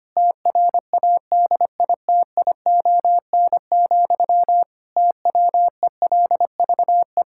Morse code, heard as a keyed tone, about 25 wpm, 700 Hz. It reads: TRADITION, TWELVE
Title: Pangram Challenge v2 - 25wpm